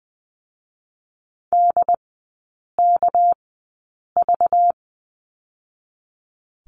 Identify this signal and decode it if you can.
Morse code, heard as a keyed tone, about 20 wpm, spaced as for 8 wpm, 700 Hz. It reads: DKV
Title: 205 Lesson 33e - Letter j - Three Characters 20wpm